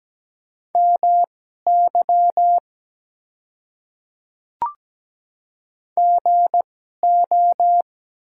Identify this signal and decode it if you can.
Morse code, heard as a keyed tone, about 17 wpm, spaced as for 12 wpm, 700 Hz. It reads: MY GO
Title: 2x Character Spacing - Top 100 Words - 17wpm